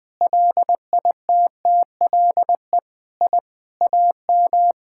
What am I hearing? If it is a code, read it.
Morse code, heard as a keyed tone, about 21 wpm, 700 Hz. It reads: LITTLE I AM
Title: Sentences from Top 1000 Words - Review Part 6 - 20wpm